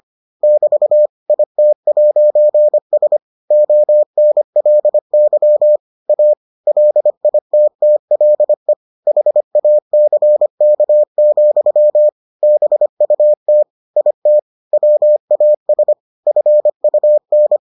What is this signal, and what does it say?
Morse code, heard as a keyed tone, about 25 wpm, 600 Hz. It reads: <BT> IT'S ONLY A LITTLE HACK, BUT IT WAS FUN